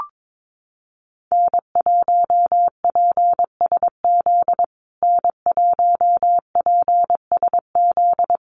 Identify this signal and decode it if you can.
Morse code, heard as a keyed tone, about 22 wpm, 700 Hz. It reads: N1PSZ N1PSZ
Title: USA Call Signs Repeated 3x 22wpm